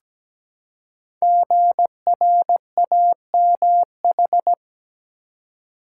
Morse code, heard as a keyed tone, about 17 wpm, 700 Hz. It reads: GRAMH
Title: Code Groups - 5 Letters - 17wpm